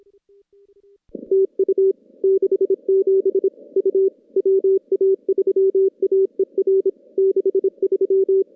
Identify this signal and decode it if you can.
Morse code, heard as a keyed tone, about 26 wpm, 390 Hz. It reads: TU 67 U WA3AER 63
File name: contests / CW-SS-k5tr- -2200